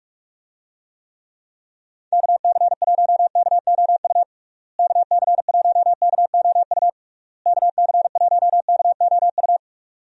Morse code, heard as a keyed tone, about 46 wpm, 700 Hz. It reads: KC1KOU KC1KOU KC1KOU
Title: USA Call Signs - 6 Characters - Repeated 3x - 45wpm